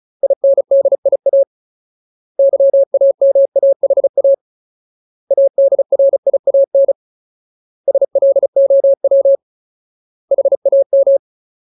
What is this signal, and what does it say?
Morse code, heard as a keyed tone, about 34 wpm, 550 Hz. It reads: INDIA YAMAHA ADRIAN SLOW HAM
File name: Ultimate Word List / 35WPM Ultimate